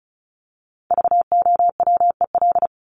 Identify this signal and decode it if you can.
Morse code, heard as a keyed tone, about 34 wpm, 700 Hz. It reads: VOWEL
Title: Top 500 Words 35wpm